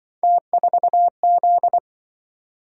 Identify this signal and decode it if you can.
Morse code, heard as a keyed tone, about 25 wpm, 700 Hz. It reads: T4Z
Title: Speed-Racing - Three Letter-Number Combinations - 20wpm